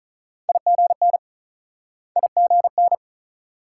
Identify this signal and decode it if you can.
Morse code, heard as a keyed tone, about 46 wpm, 700 Hz. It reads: IGN IGN